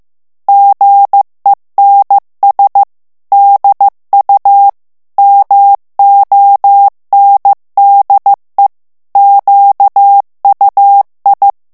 Morse code, heard as a keyed tone, about 15 wpm, 800 Hz. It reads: GENS DU MONDE QUI
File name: morse mp3 / FRANCAIS TEXTE CLAIR 2 VITESSE 14 MOTS MINUTES